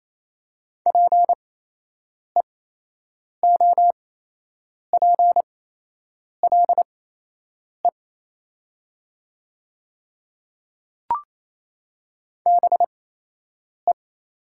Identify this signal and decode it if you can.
Morse code, heard as a keyed tone, about 28 wpm, spaced as for 8 wpm, 700 Hz. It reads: PEOPLE BE
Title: 8x Character Spacing - Top 100 Words - 28wpm